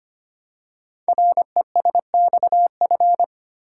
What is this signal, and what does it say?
Morse code, heard as a keyed tone, about 25 wpm, 700 Hz. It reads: RESXF
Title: Code Groups - 5 Letters - 25wpm